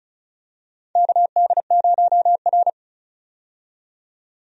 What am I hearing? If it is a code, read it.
Morse code, heard as a keyed tone, about 36 wpm, 700 Hz. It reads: KD0R
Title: USA Call Signs - 4 Characters - 35wpm